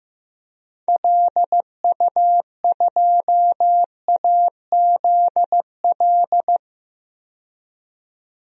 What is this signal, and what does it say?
Morse code, heard as a keyed tone, about 15 wpm, 700 Hz. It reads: LU2AZL
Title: DXCC-100 Call Signs - 6 Characters - 15wpm